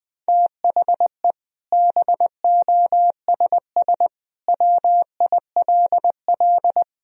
Morse code, heard as a keyed tone, about 20 wpm, 700 Hz. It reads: THE BOSS WILL
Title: Sentences from Top 1000 Words - Review Part 4 - 20wpm